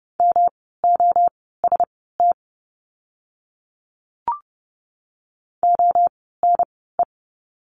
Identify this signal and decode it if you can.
Morse code, heard as a keyed tone, about 30 wpm, spaced as for 17 wpm, 700 Hz. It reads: MOST ONE
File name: course / 30:10wpm / 115 Lesson 19f - Letter w - Words 30wpm